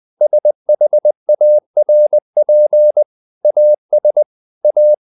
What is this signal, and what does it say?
Morse code, heard as a keyed tone, about 20 wpm, 600 Hz. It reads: SHARP AS A
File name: the old man and the sea / the old man and the sea part 1 20wpm